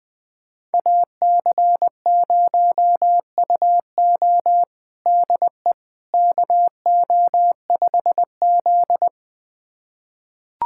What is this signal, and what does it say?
Morse code, heard as a keyed tone, about 20 wpm, 700 Hz. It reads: AC0UO DE KO5Z
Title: USA Callsign DE Callsign 20wpm